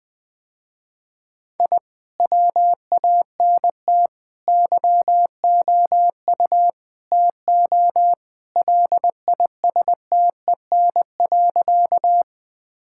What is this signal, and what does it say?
Morse code, heard as a keyed tone, about 21 wpm, 700 Hz. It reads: I WANT YOU TO LISTEN.
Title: Sentences with 4-Word Phrases - Spoken Beginning - 20wpm